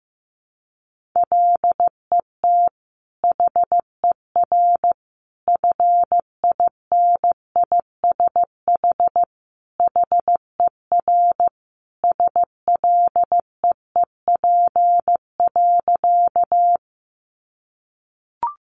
Morse code, heard as a keyed tone, about 15 wpm, 700 Hz. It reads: LET HER FINISH HER SLEEP.
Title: Sentences from Top 1200 Words 15wpm